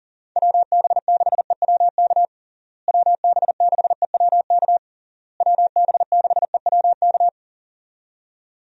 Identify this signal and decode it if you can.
Morse code, heard as a keyed tone, about 40 wpm, 700 Hz. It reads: WB6EWK WB6EWK WB6EWK
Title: USA Call Signs - 6 Characters - Repeated 3x - 40wpm